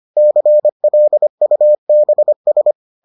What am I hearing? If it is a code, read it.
Morse code, heard as a keyed tone, about 25 wpm, 600 Hz. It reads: CLUBS